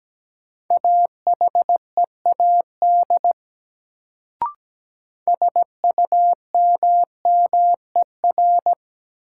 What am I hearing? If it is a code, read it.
Morse code, heard as a keyed tone, about 17 wpm, 700 Hz. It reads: AHEAD SUMMER
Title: Sentences from Top 1100 Words - New Word Drill Alt - 17wpm